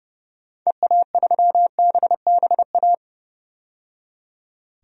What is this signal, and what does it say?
Morse code, heard as a keyed tone, about 30 wpm, 700 Hz. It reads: EA3BBA
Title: DXCC-100 Call Signs - 6 Characters - 30wpm